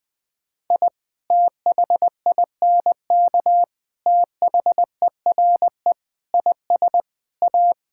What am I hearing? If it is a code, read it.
Morse code, heard as a keyed tone, about 20 wpm, 700 Hz. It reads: I THINK THERE IS A
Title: Sentences with 5-Word Phrases - 20wpm